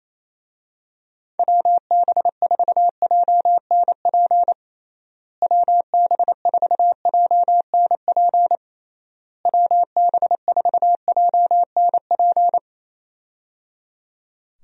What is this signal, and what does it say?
Morse code, heard as a keyed tone, about 28 wpm, 700 Hz. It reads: WB4JNP WB4JNP WB4JNP
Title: USA Call Signs - 6 Characters - Repeated 3x - 28wpm